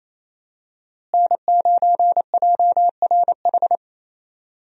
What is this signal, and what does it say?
Morse code, heard as a keyed tone, about 28 wpm, 700 Hz. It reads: N9JRH
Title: USA Call Signs - 5 Characters - Repeated 2x - 28wpm